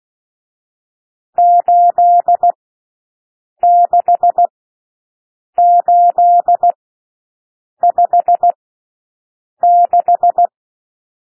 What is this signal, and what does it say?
Morse code, heard as a keyed tone, about 16 wpm, spaced as for 6 wpm, 700 Hz. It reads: 86856
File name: SOUND NUMBERS 16 wpm